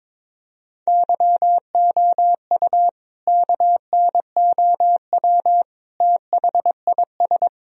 Morse code, heard as a keyed tone, about 22 wpm, 700 Hz. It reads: YOU KNOW THIS
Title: Sentences from Top 1000 Words - Review Part 2 - 22wpm